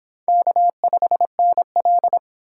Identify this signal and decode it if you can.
Morse code, heard as a keyed tone, about 26 wpm, 700 Hz. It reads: K5NL